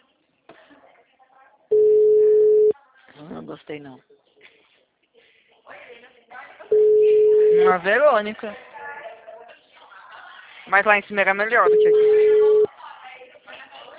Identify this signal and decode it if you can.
Ringback tone (Europe, 425 Hz), with speech.